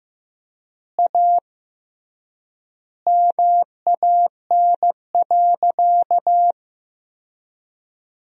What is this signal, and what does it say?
Morse code, heard as a keyed tone, about 15 wpm, 700 Hz. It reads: A MAN.
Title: 3x Word-Spacing - Sentences from Top 300 Words - 15wpm